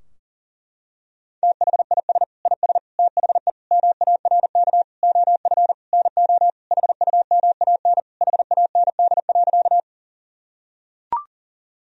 Morse code, heard as a keyed tone, about 40 wpm, 700 Hz. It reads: THIS IS THE MARK OF NO HUMAN HAND.
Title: Sentences from Top 1000 Words - Review Part 3 - 40wpm